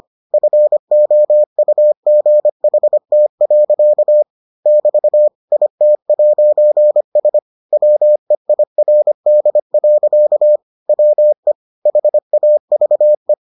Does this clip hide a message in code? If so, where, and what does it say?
Yes: Morse code from 0.3 to 13.3 s, heard as a keyed tone, about 25 wpm, 600 Hz. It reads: FOUGHT. <BT> IT'S WEIRD. WE HAVE